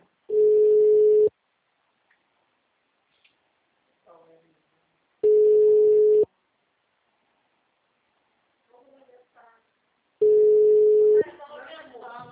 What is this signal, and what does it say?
Ringback tone (Europe, 425 Hz), with speech.